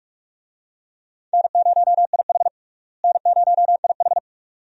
Morse code, heard as a keyed tone, about 46 wpm, 700 Hz. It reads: N0IH N0IH
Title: USA Call Signs - 4 Characters - Repeated 2x - 45wpm